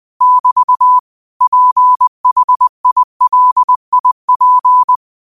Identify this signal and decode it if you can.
Morse code, heard as a keyed tone, about 20 wpm, 1000 Hz. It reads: <BT> PHILIP
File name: qotdfiles / 20wpm